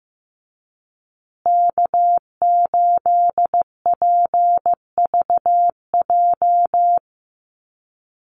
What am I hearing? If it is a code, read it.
Morse code, heard as a keyed tone, about 15 wpm, 700 Hz. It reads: K8PVJ
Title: USA Call Signs Repeated 3x 15wpm